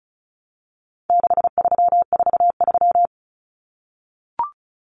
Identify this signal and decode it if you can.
Morse code, heard as a keyed tone, about 34 wpm, 700 Hz. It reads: 6343
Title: Numbers - 4-Digit 35wpm